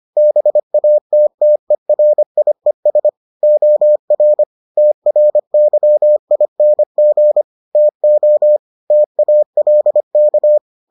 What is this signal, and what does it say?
Morse code, heard as a keyed tone, about 25 wpm, 600 Hz. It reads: BATTERIES OR TRYING TO TALK